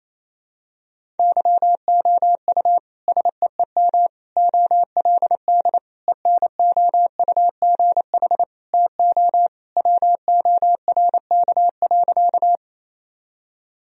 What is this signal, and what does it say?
Morse code, heard as a keyed tone, about 28 wpm, 700 Hz. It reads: YOU SEEM OLD ENOUGH TO WORK.
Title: Sentences from Top 1000 Words - Review Part 1 - 28wpm